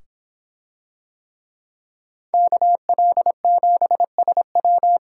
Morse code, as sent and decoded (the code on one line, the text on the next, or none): -.- .-.. --... ... .--
KL7SW